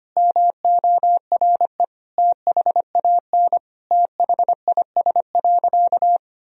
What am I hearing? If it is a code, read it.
Morse code, heard as a keyed tone, about 25 wpm, 700 Hz. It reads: MORE THAN THIS.
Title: Sentences from Top 1000 Words - Review Part 1 - 25wpm